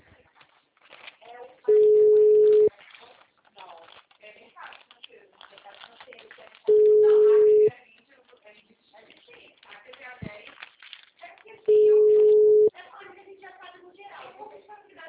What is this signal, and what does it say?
Ringback tone (Europe, 425 Hz), with speech.